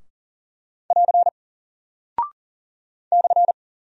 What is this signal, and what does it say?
Morse code, heard as a keyed tone, about 40 wpm, 700 Hz. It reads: <AR> /
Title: Single Character (Letter:Number:Punctuation:Prosign) Alt 40wpm